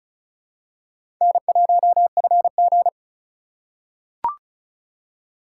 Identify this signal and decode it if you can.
Morse code, heard as a keyed tone, about 34 wpm, 700 Hz. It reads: N1FG
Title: USA Call Signs - 4 Characters - 4x Character Spacing - 35wpm